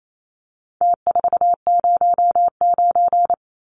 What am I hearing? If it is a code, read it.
Morse code, heard as a keyed tone, about 28 wpm, 700 Hz. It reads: T409
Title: Numbers - Contest Style - 4-Digit 28wpm